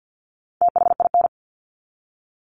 Morse code, heard as a keyed tone, about 50 wpm, 700 Hz. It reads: THIN
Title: Top 1001-1250 Words 50wpm